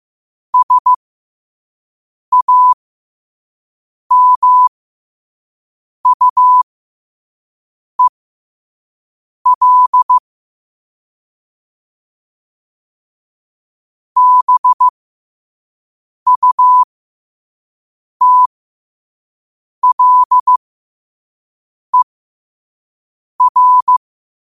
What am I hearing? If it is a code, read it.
Morse code, heard as a keyed tone, about 15 wpm, spaced as for 5 wpm, 1000 Hz. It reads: SAMUEL BUTLER